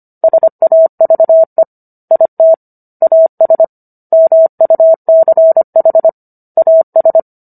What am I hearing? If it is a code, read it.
Morse code, heard as a keyed tone, about 25 wpm, 650 Hz. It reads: SAVE IT AS MUCH AS